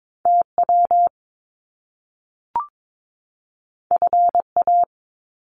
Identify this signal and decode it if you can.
Morse code, heard as a keyed tone, about 22 wpm, 700 Hz. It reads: TW FA
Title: Top Two Letter Combinations in Words 22wpm